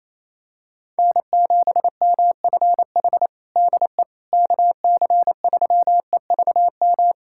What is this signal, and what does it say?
Morse code, heard as a keyed tone, about 28 wpm, 700 Hz. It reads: N7MFH DE KC3EVM
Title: USA Callsign DE Callsign 28wpm